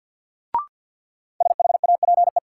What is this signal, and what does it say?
Morse code, heard as a keyed tone, about 52 wpm, 700 Hz. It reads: SHAPE
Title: Sets of 5 Words Encore - New Word Drill - 50wpm